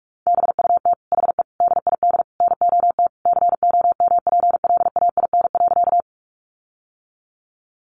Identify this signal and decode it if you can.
Morse code, heard as a keyed tone, about 44 wpm, 700 Hz. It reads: BUT HE DID NOT COMPLAIN.
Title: Sentences from Top 1300 Words 45wpm